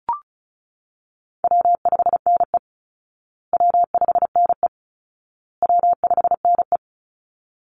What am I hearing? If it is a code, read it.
Morse code, heard as a keyed tone, about 34 wpm, 700 Hz. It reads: W5NE W5NE W5NE
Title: Easy Call Signs Repeated 3x - 35wpm